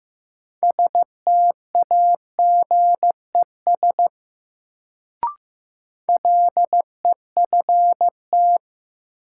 Morse code, heard as a keyed tone, about 15 wpm, 700 Hz. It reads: STAGES LEFT
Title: Sets of 5 Words Encore - New Word Drill - 15wpm